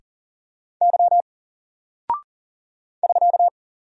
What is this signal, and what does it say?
Morse code, heard as a keyed tone, about 40 wpm, 700 Hz. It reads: Y <SK>
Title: Single Character (Letter:Number:Punctuation:Prosign) 40wpm